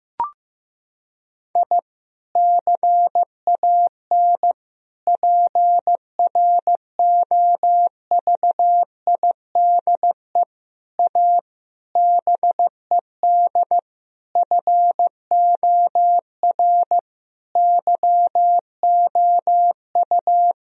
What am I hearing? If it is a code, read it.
Morse code, heard as a keyed tone, about 15 wpm, 700 Hz. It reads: I CAN PROVIDE A BED FOR YOU
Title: Sentences from Top 1000 Words - Review Part 3 - 15wpm